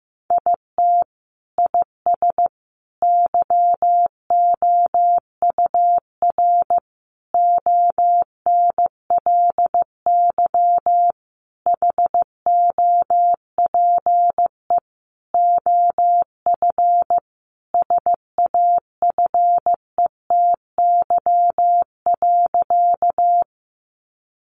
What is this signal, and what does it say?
Morse code, heard as a keyed tone, about 15 wpm, 700 Hz. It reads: IT IS YOUR ONLY HOPE OF SAFETY.
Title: Sentences from Top 1000 Words 15wpm